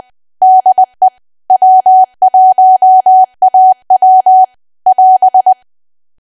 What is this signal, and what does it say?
Morse code, heard as a keyed tone, about 20 wpm, 750 Hz. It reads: DE W1AW <AS>
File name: cwa / W1AW-Level3 / 20WPM